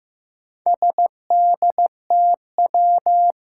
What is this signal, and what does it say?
Morse code, heard as a keyed tone, about 15 wpm, 700 Hz. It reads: SDTW
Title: Four Challenging Letter-Number Combinations 15wpm